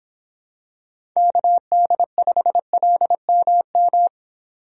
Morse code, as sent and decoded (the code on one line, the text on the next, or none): -.- -.. ..... .-.. -- --
KD5LMM